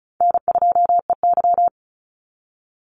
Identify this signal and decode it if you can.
Morse code, heard as a keyed tone, about 34 wpm, 700 Hz. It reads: N2EY
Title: Easy Call Signs Repeated 3x - 35wpm